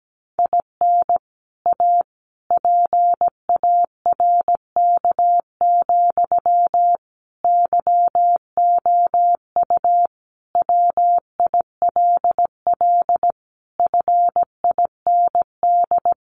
Morse code, heard as a keyed tone, about 17 wpm, 700 Hz. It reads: IN A PARK, YOU WILL FIND
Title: Sentences from Top 1300 Words 17wpm